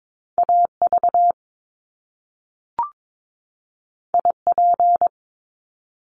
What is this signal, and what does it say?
Morse code, heard as a keyed tone, about 22 wpm, 700 Hz. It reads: AV IP
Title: Top Two Letter Combinations in Words 22wpm